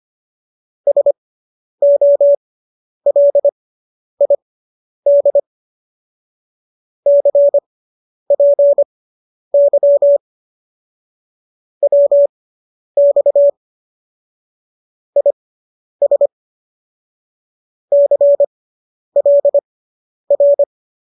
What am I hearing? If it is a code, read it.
Morse code, heard as a keyed tone, about 25 wpm, spaced as for 10 wpm, 570 Hz. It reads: SOLID CPY WX IS CLR